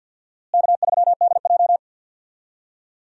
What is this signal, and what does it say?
Morse code, heard as a keyed tone, about 52 wpm, 700 Hz. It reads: K3DJ
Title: USA Call Signs - 4 Characters - Repeated 2x - 50wpm